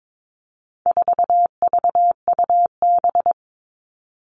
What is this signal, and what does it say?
Morse code, heard as a keyed tone, about 22 wpm, 700 Hz. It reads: 4VUB
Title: All Cut Numbers - 4-Digit 22wpm